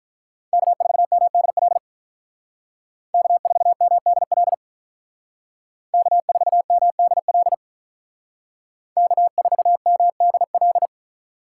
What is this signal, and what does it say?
Morse code, heard as a keyed tone, about 43 wpm, 700 Hz. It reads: K4MDL K4MDL K4MDL K4MDL
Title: Speed Racing - USA 1x3 Call Signs 35wpm